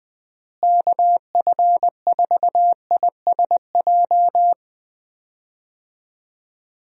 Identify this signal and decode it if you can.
Morse code, heard as a keyed tone, about 20 wpm, 700 Hz. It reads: KF4ISJ
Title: USA Call Signs - 6 Characters - Repeated 3x - 20wpm